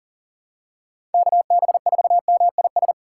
Morse code, heard as a keyed tone, about 40 wpm, 700 Hz. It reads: KB4MIS
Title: USA Call Signs - 6 Characters - 3x Character Spacing - 40wpm